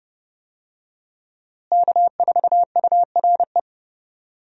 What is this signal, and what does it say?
Morse code, heard as a keyed tone, about 30 wpm, 700 Hz. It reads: K4URE
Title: USA Call Signs - 5 Characters - Repeated 3x - 30wpm